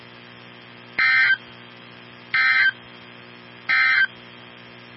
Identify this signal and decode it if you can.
Emergency Alert System (SAME) tones, heard as harsh buzzing bursts. It is heard from 0.9 to 4.1 s.
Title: NOAA Weather Radio 162.550